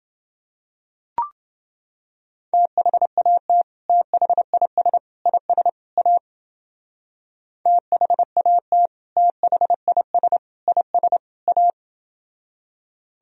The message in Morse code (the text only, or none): THAT THIS IS A THAT THIS IS A